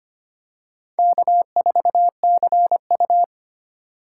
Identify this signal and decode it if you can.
Morse code, heard as a keyed tone, about 25 wpm, 700 Hz. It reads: K4CU
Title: USA Call Signs - 4 Characters - 3x Character Spacing - 25wpm